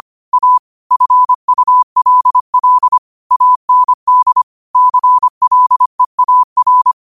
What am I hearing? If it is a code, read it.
Morse code, heard as a keyed tone, about 25 wpm, 1000 Hz. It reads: A FULL AND CLEAR
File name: qotdfiles / 25wpm